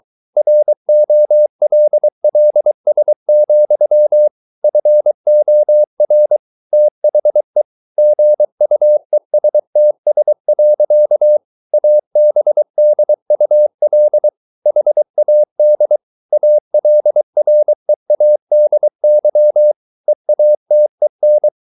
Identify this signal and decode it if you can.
Morse code, heard as a keyed tone, about 23 wpm, 600 Hz. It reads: ROLLS, FOR THE GUESTS. ABDUL HAD ALREADY EATEN